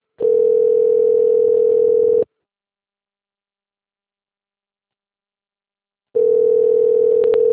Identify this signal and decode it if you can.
Ringback tone (North America), 440 + 480 Hz.